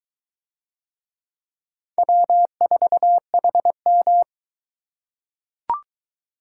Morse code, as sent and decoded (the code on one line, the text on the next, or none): .-- ....- .... --
W4HM